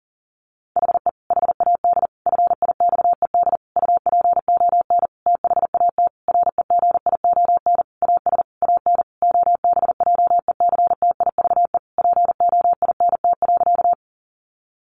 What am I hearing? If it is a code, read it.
Morse code, heard as a keyed tone, about 40 wpm, 700 Hz. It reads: HE HAD FIXED UPON THAT REGION AS AN OBJECTIVE POINT.
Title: Sentences from Top 1500 Words 40wpm